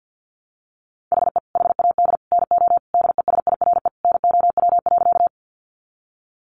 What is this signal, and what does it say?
Morse code, heard as a keyed tone, about 48 wpm, 700 Hz. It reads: HE HAD NO DESIRE NOW.
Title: Sentences from Top 1100 Words 50wpm